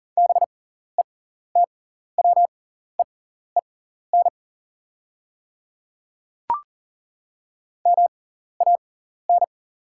Morse code, heard as a keyed tone, about 40 wpm, spaced as for 14 wpm, 690 Hz. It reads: BETWEEN MAN
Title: 6x Character Spacing - Top 500 Words - 40wpm